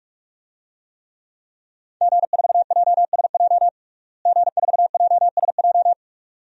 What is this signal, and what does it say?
Morse code, heard as a keyed tone, about 46 wpm, 700 Hz. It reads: G4JSJ G4JSJ
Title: DXCC-100 Call Signs - 5 Characters - Repeated 2x - 45wpm